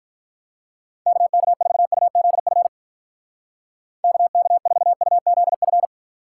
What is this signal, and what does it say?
Morse code, heard as a keyed tone, about 44 wpm, 700 Hz. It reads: KK4UZF KK4UZF